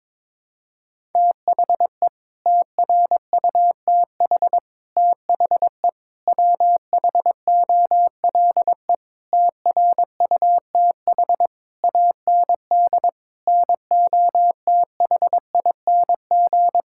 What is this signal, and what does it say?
Morse code, heard as a keyed tone, about 22 wpm, 700 Hz. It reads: THE TRUTH THE WHOLE TRUTH AND NOTHING
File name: Sets of 9 Words - 22wpm